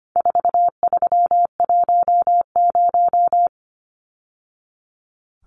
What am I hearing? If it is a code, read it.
Morse code, heard as a keyed tone, about 25 wpm, 700 Hz. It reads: 4310
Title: Numbers - 4-Digit 25wpm